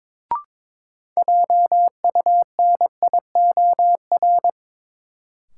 Morse code, heard as a keyed tone, about 22 wpm, 700 Hz. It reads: JUNIOR
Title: Sentences from Top 1000 Words - Review Part 7 - New Word Drill - 22wpm